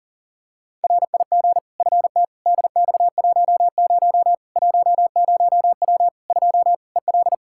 Morse code, heard as a keyed tone, about 37 wpm, 700 Hz. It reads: RIG FT DX10 10W 2 EL
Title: Practice QSO - Exchange 2 - Info - 40wpm